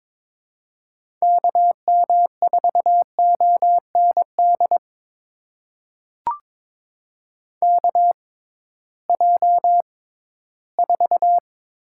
Morse code, heard as a keyed tone, about 22 wpm, spaced as for 14 wpm, 700 Hz. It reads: KM4OND KJ4
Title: USA Call Signs - 6 Characters - 6x Character Spacing - 22wpm